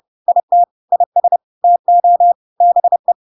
Morse code, heard as a keyed tone, about 30 wpm, 700 Hz. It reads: IT IS TO BE
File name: Pinocchio / Pinocchio ch5 30 30 ik1hge